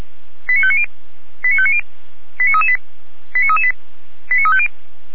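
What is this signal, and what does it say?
Five-tone selective call (ZVEI), heard as quick runs of five tones, tones of about 70 ms. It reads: 89590 89590 89338 89338 89350
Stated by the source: Recorded with an SDR, ZVEI five-tone selective call.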